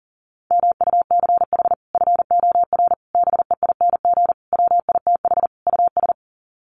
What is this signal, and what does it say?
Morse code, heard as a keyed tone, about 40 wpm, 700 Hz. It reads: MUCH FOR BEING WITH US
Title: Sets of 5 Words 40wpm